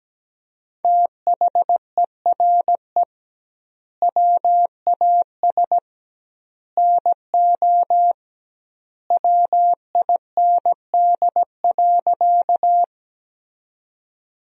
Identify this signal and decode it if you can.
Morse code, heard as a keyed tone, about 17 wpm, 700 Hz. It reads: THERE WAS NO WIND.
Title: 2x Word-Spacing - Sentences from Top 400 Words - 17wpm